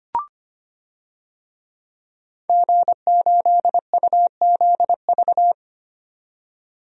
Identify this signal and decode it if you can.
Morse code, heard as a keyed tone, about 25 wpm, 700 Hz. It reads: G8UZV
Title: DXCC-100 Call Signs - 5 Characters - 25wpm